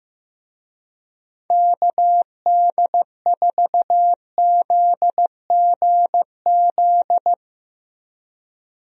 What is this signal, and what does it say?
Morse code, heard as a keyed tone, about 15 wpm, 700 Hz. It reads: KD4ZGZ
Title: USA Call Signs - 6 Characters - Repeated 3x - 15wpm